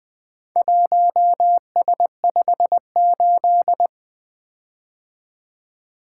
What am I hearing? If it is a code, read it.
Morse code, heard as a keyed tone, about 24 wpm, 700 Hz. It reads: 1S58
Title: Speed-Racing - Four Letter-Number Combinations - 20wpm